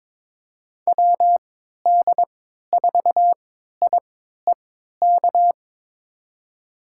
Morse code, heard as a keyed tone, about 21 wpm, spaced as for 12 wpm, 700 Hz. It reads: WD4IEK
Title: USA Call Signs - 6 Characters - 3x Character Spacing - 22wpm